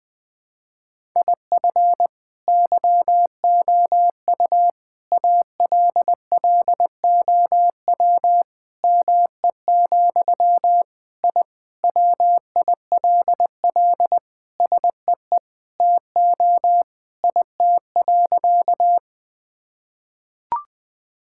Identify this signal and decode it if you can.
Morse code, heard as a keyed tone, about 20 wpm, 700 Hz. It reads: IF YOU ALLOW ME, I WILL SEE TO IT.
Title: Sentences from Top 1000 Words - Review Part 5 - 20wpm